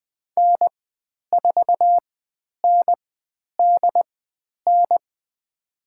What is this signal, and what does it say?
Morse code, heard as a keyed tone, about 20 wpm, spaced as for 10 wpm, 700 Hz. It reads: N4NDN
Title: 095 Lesson 16g - Letter c - Call Signs 20wpm